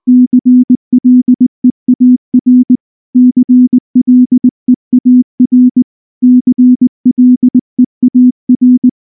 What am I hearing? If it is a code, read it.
Morse code, heard as a keyed tone, about 20 wpm, 250 Hz. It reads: CLEAR CLEAR CLEAR
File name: morse / morse-training-20WPM-7